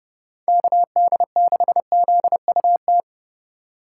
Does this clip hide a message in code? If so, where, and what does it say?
Yes: Morse code from 0.5 to 3.0 s, heard as a keyed tone, about 30 wpm, 700 Hz. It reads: KD6ZUT